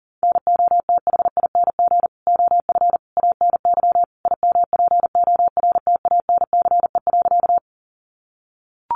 Morse code, heard as a keyed tone, about 40 wpm, 700 Hz. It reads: NOTHING OF ANY IMPORTANCE.
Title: Sentences from Top 1200 Words 40wpm